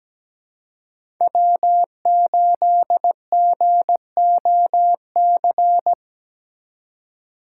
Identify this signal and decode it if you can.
Morse code, heard as a keyed tone, about 17 wpm, 700 Hz. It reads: W8GOC
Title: USA Call Signs - 5 Characters - 4x Character Spacing - 17wpm